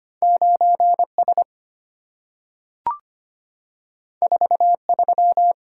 Morse code, heard as a keyed tone, about 25 wpm, 700 Hz. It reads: 9S 43
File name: Two Letter-Number Combinations - 25wpm